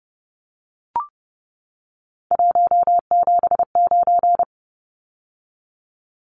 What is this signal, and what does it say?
Morse code, heard as a keyed tone, about 30 wpm, 700 Hz. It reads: 179
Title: Numbers - 3-Digit 30wpm